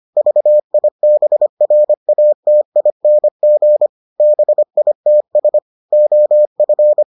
Morse code, heard as a keyed tone, about 25 wpm, 600 Hz. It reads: VIBRATING BITS OF